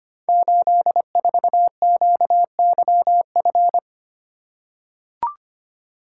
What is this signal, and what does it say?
Morse code, heard as a keyed tone, about 25 wpm, 700 Hz. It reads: 84QYF
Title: Code Groups - 5 Letter:Numbers - 25wpm